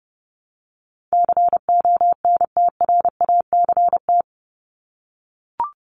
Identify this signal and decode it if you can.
Morse code, heard as a keyed tone, about 30 wpm, 700 Hz. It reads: CONTRACT
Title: Top 1251-1500 Words 30wpm